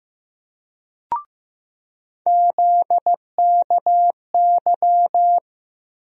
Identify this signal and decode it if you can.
Morse code, heard as a keyed tone, about 15 wpm, 700 Hz. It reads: ZKY
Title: USA Call Sign Postfixes - 3 Letters 15wpm